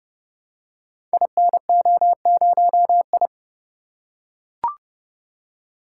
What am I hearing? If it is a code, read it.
Morse code, heard as a keyed tone, about 30 wpm, 700 Hz. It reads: INO0I
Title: Code Groups - 5 Letter:Numbers - 30wpm